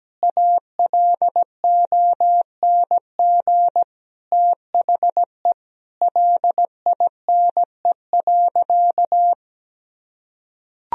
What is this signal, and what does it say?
Morse code, heard as a keyed tone, about 17 wpm, 700 Hz. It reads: ALONG THE LINE.
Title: Sentences from Top 1000 Words - Review Part 1 - 17wpm